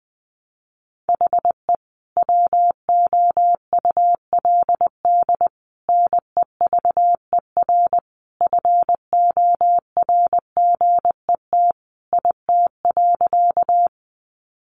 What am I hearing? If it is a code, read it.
Morse code, heard as a keyed tone, about 20 wpm, 700 Hz. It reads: HE WOULD NEVER FORGET IT.
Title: Sentences from Top 1200 Words 20wpm